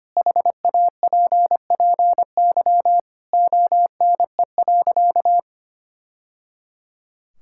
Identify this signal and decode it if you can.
Morse code, heard as a keyed tone, about 25 wpm, 700 Hz. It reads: HAPPY ONE.
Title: Sentences from Top 1000 Words - Review Part 5 - 25wpm